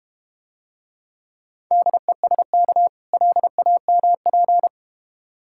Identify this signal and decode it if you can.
Morse code, heard as a keyed tone, about 30 wpm, 700 Hz. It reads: DESK LAMP
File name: practice / 25wpm / Difficult Words - Speed-Racing with Hints - 25wpm